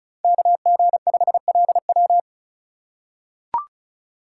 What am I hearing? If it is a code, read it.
Morse code, heard as a keyed tone, about 34 wpm, 700 Hz. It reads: KG5LW
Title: USA Call Signs - 5 Characters - 3x Character Spacing - 35wpm